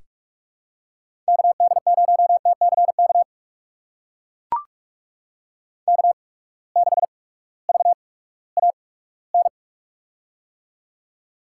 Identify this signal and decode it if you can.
Morse code, heard as a keyed tone, about 42 wpm, spaced as for 20 wpm, 700 Hz. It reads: KD0TCK K6VAN
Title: USA Call Signs - 8x Character Spacing - 45wpm